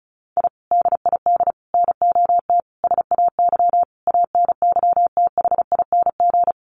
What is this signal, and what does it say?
Morse code, heard as a keyed tone, about 34 wpm, 700 Hz. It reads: I DID NOT SAY ANYTHING
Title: Sets of 5 Words Encore2 35wpm